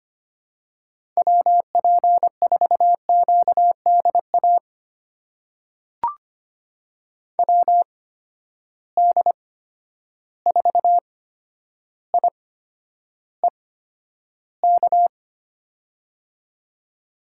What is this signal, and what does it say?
Morse code, heard as a keyed tone, about 25 wpm, spaced as for 11 wpm, 700 Hz. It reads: WP4QDA WD4IEK
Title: USA Call Signs - 6 Characters - 8x Character Spacing - 25wpm